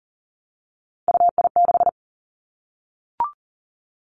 Morse code, heard as a keyed tone, about 38 wpm, 690 Hz. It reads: UI6
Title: Three Challenging Letter-Number Combinations 40wpm